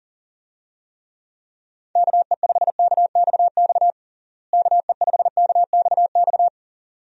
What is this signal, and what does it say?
Morse code, heard as a keyed tone, about 40 wpm, 700 Hz. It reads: KE5KXX KE5KXX
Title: USA Call Signs - 6 Characters - Repeated 2x - 40wpm